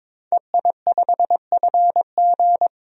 Morse code, heard as a keyed tone, about 22 wpm, 700 Hz. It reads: EI5FG
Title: DXCC-100 Call Signs - 5 Characters - 22wpm